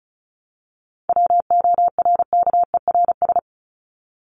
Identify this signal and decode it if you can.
Morse code, heard as a keyed tone, about 34 wpm, 700 Hz. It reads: WORKERS
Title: ES:S Suffix - Top 100 Words with Base Words 35wpm